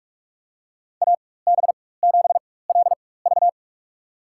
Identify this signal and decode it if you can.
Morse code, heard as a keyed tone, about 46 wpm, spaced as for 21 wpm, 700 Hz. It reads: AB7LV
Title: USA Call Signs - 4x Character Spacing - 45wpm